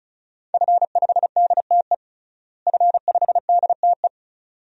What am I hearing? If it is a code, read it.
Morse code, heard as a keyed tone, about 36 wpm, 700 Hz. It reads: F5DTE F5DTE
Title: DXCC-100 Call Signs - 5 Characters - Repeated 2x - 35wpm